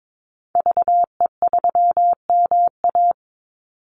Morse code, heard as a keyed tone, about 22 wpm, 690 Hz. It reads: VE3MA
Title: Call Signs Repeated 3x - 22wpm